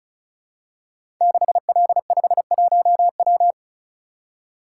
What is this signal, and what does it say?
Morse code, heard as a keyed tone, about 36 wpm, 700 Hz. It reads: 6L51W
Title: Code Groups - 5 Letter:Numbers - 35wpm